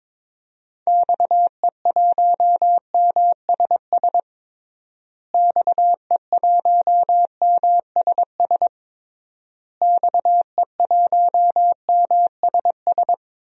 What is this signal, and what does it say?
Morse code, heard as a keyed tone, about 22 wpm, 700 Hz. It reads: XE1MSS XE1MSS XE1MSS
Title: DXCC-100 Call Signs - 6 Characters - Repeated 3x - 22wpm